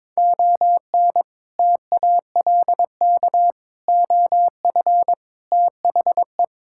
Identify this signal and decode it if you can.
Morse code, heard as a keyed tone, about 22 wpm, 700 Hz. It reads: ON TALK OF THE
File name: practice / 22wpm / Build A Phrase - 5-Word Phrases - 22wpm